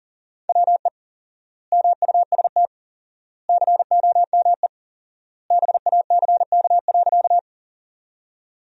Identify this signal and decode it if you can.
Morse code, heard as a keyed tone, about 37 wpm, 700 Hz. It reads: WE MUST COME BACK.
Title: 4x Word-Spacing - Sentences from Top 200 Words - 40wpm